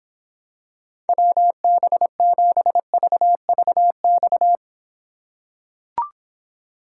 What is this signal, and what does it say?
Morse code, heard as a keyed tone, about 26 wpm, 700 Hz. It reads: WB7VVX